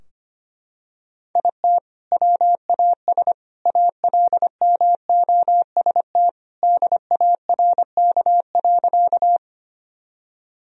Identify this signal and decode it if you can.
Morse code, heard as a keyed tone, about 25 wpm, 700 Hz. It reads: IT WAS ALMOST DARK.
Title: Sentences from Top 1000 Words - Review Part 4 - 25wpm